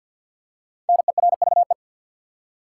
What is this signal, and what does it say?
Morse code, heard as a keyed tone, about 48 wpm, 700 Hz. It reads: NERVE
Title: Five Letter Words - 50wpm